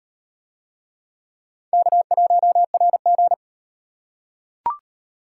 Morse code, heard as a keyed tone, about 35 wpm, 700 Hz. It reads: K1RG